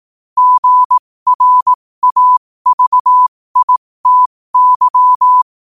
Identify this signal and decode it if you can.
Morse code, heard as a keyed tone, about 18 wpm, spaced as for 15 wpm, 1000 Hz. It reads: GRAVITY
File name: qotdfiles / 15wpm